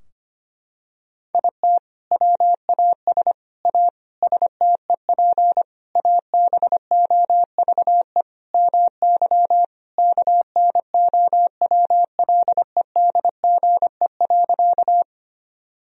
Morse code, heard as a keyed tone, about 25 wpm, 700 Hz. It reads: IT WAS A STEP ABOVE MY KNOWLEDGE.
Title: Sentences from Top 1000 Words - Review Part 3 - 25wpm